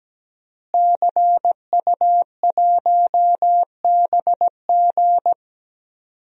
Morse code, heard as a keyed tone, about 17 wpm, 700 Hz. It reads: CU1BG
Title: DXCC-100 Call Signs - 5 Characters - Repeated 2x - 17wpm